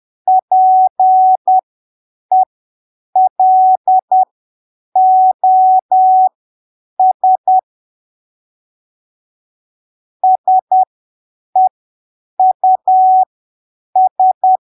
Morse code, heard as a keyed tone, about 10 wpm, spaced as for 7 wpm, 750 Hz. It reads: PELOS SEUS
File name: arquivos / texto / texto 06 10ppm